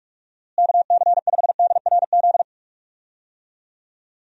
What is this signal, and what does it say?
Morse code, heard as a keyed tone, about 48 wpm, 700 Hz. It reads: KC5DRZ
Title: USA Call Signs - 6 Characters - Repeated 2x - 45wpm